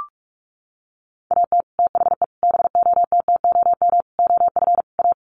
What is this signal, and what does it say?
Morse code, heard as a keyed tone, about 44 wpm, 690 Hz. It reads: AT THE BOTTOM OF A
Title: Sets of 5 Words Encore2 45wpm